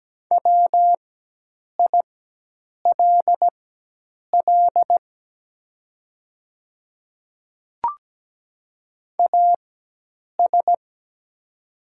Morse code, heard as a keyed tone, about 17 wpm, spaced as for 8 wpm, 690 Hz. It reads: WILL AS
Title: 4x Character Spacing - Top 100 Words - 17wpm